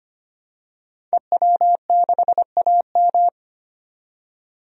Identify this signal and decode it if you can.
Morse code, heard as a keyed tone, about 25 wpm, 700 Hz. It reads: EW6AM
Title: DXCC-100 Call Signs - 5 Characters - 25wpm